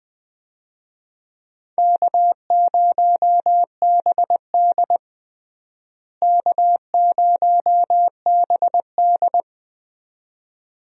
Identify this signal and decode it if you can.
Morse code, heard as a keyed tone, about 20 wpm, 700 Hz. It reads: K0BD K0BD
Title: USA Call Signs - 4 Characters - Repeated 2x - 20wpm